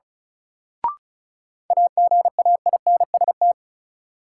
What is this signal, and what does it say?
Morse code, heard as a keyed tone, about 36 wpm, 700 Hz. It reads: AGAINST
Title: Sets of 2 Words - New Word Drill - 35wpm